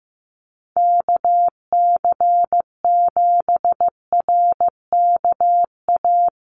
Morse code, heard as a keyed tone, about 15 wpm, 700 Hz. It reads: KC7RKA
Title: USA Call Signs 15wpm